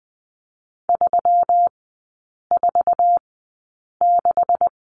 Morse code, heard as a keyed tone, about 20 wpm, spaced as for 8 wpm, 700 Hz. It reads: 346
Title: 127 Lesson 21e - Number 6 - Three Characters 20wpm